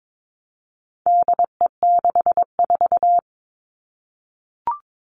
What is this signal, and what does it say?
Morse code, heard as a keyed tone, about 22 wpm, 700 Hz. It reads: DE64
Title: All Cut Numbers - 4-Digit 22wpm